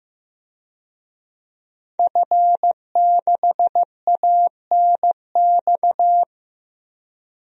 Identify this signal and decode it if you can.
Morse code, heard as a keyed tone, about 15 wpm, 700 Hz. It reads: F6ANX
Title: DXCC-100 Call Signs - 5 Characters - Repeated 2x - 15wpm